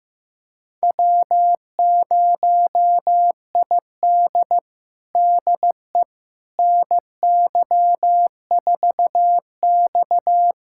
Morse code, heard as a keyed tone, about 16 wpm, 700 Hz. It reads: W0ID DE NY4X
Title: USA Callsign DE Callsign - 4 Characters - 15wpm